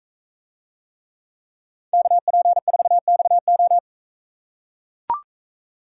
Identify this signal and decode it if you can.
Morse code, heard as a keyed tone, about 41 wpm, 700 Hz. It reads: KP4XO